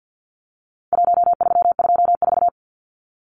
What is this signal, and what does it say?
Morse code, heard as a keyed tone, about 50 wpm, 700 Hz. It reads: 1324
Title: Numbers - Contest Style - 4-Digit 50wpm